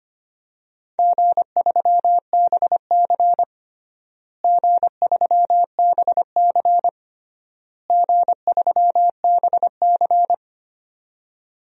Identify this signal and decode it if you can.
Morse code, heard as a keyed tone, about 25 wpm, 700 Hz. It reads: G3BC G3BC G3BC
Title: DXCC-100 Call Signs - 4 Characters - Repeated 3x - 25wpm